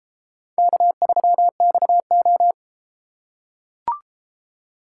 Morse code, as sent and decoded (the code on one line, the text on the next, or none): -.- ...-- -..- ---
K3XO